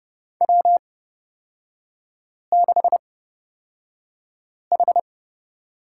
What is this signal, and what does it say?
Morse code, heard as a keyed tone, about 31 wpm, spaced as for 5 wpm, 690 Hz. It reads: W6H
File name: Lesson 37e - Letter x - Three Characters 30wpm